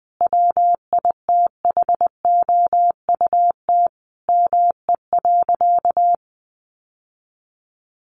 Morse code, heard as a keyed tone, about 20 wpm, 700 Hz. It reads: WITHOUT ME.
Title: Sentences from Top 1200 Words 20wpm